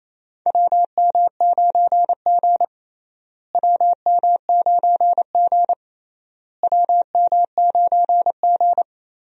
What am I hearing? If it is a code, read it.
Morse code, heard as a keyed tone, about 27 wpm, 700 Hz. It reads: WM9G WM9G WM9G
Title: USA Call Signs - 4 Characters - Repeated 3x - 28wpm